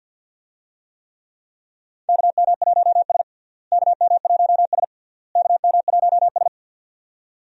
Morse code, heard as a keyed tone, about 53 wpm, 700 Hz. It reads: KM1S KM1S KM1S
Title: USA Call Signs - 4 Characters - Repeated 3x - 50wpm